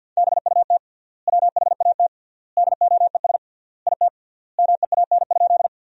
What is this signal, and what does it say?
Morse code, heard as a keyed tone, about 50 wpm, 700 Hz. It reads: BUT WHAT DOES IT MEAN?
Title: 3x Word-Spacing - Sentences from Top 200 Words - 50wpm